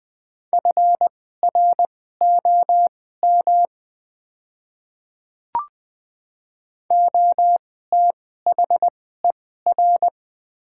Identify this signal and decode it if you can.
Morse code, heard as a keyed tone, about 20 wpm, spaced as for 14 wpm, 700 Hz. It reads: FROM OTHER
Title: 2x Character Spacing - Top 200 Words - 20wpm